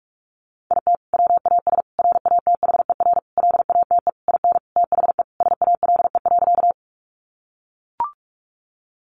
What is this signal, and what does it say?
Morse code, heard as a keyed tone, about 44 wpm, 700 Hz. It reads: IT WAS RATHER LATE IN THE SALE.
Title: Sentences from Top 1400 Words 45wpm